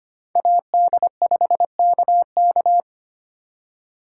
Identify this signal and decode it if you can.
Morse code, heard as a keyed tone, about 32 wpm, 700 Hz. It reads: AD5KK